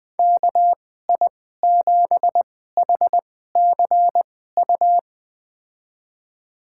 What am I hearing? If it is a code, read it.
Morse code, heard as a keyed tone, about 20 wpm, spaced as for 14 wpm, 700 Hz. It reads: KI7HCU
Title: USA Call Signs - 6 Characters - 2x Character Spacing - 20wpm